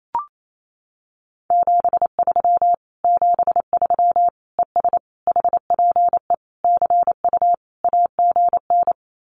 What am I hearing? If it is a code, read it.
Morse code, heard as a keyed tone, about 28 wpm, 700 Hz. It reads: 73 73 ES HPE CU AGN
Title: Common QSO Phrases Encore 28wpm